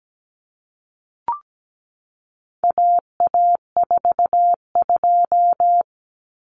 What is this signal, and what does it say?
Morse code, heard as a keyed tone, about 17 wpm, 700 Hz. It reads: AA42
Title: Common Cut Numbers - 4-Digit 17wpm